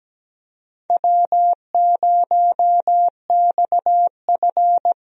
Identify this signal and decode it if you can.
Morse code, heard as a keyed tone, about 17 wpm, 700 Hz. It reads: W0XF
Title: USA Call Signs - 4 Characters - Repeated 3x - 17wpm